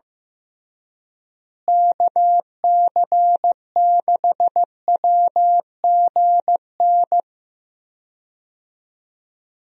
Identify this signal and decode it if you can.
Morse code, heard as a keyed tone, about 15 wpm, 700 Hz. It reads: KC6WGN
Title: USA Call Signs - 6 Characters - Repeated 2x - 15wpm